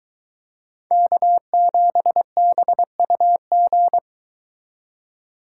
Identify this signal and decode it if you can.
Morse code, heard as a keyed tone, about 22 wpm, 700 Hz. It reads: K7BUG